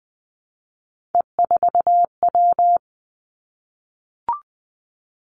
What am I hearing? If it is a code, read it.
Morse code, heard as a keyed tone, about 20 wpm, 700 Hz. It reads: E4W
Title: Three Challenging Letter-Number Combinations 20wpm